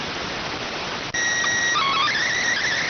SSTV (slow-scan television), PD 290.